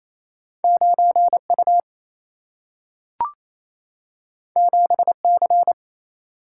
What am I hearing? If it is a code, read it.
Morse code, heard as a keyed tone, about 28 wpm, 700 Hz. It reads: 9 U 7 C
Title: Single-Letter Single-Number 28wpm